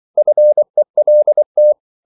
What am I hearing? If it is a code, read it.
Morse code, heard as a keyed tone, about 24 wpm, 600 Hz. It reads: FELT
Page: Korte danske ord 24wpm